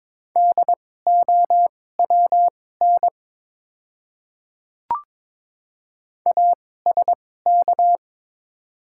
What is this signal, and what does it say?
Morse code, heard as a keyed tone, about 22 wpm, spaced as for 16 wpm, 700 Hz. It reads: DOWN ASK
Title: 2x Character Spacing - Top 200 Words - 22wpm